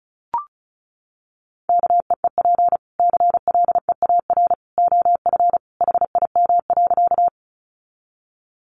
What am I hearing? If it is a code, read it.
Morse code, heard as a keyed tone, about 34 wpm, 700 Hz. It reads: KEEP CLEAR OF HIM.
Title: Sentences from Top 500 Words 35wpm